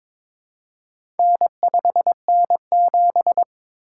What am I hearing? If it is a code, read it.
Morse code, heard as a keyed tone, about 22 wpm, 700 Hz. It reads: N5N7
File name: practice / 22wpm / Four Letter-Number Combinations - 22wpm